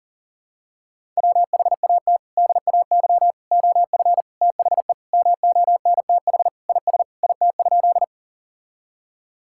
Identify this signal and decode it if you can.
Morse code, heard as a keyed tone, about 37 wpm, 700 Hz. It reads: WHAT DAY OF THE MONTH IS IT?
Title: Sentences from Top 1000 Words - Review Part 5 - 40wpm